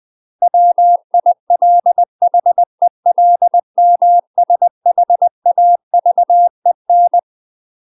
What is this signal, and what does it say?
Morse code, heard as a keyed tone, about 20 wpm, 700 Hz. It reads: WILHELMSHAVEN
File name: UK MASTER QSOs TRAINING / 20WPM / UK MASTER QSOs 0026